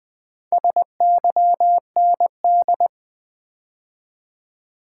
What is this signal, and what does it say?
Morse code, heard as a keyed tone, about 20 wpm, 700 Hz. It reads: SYND
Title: Build A Word - 4 Letters At A Time - 8-Letter Words - 20wpm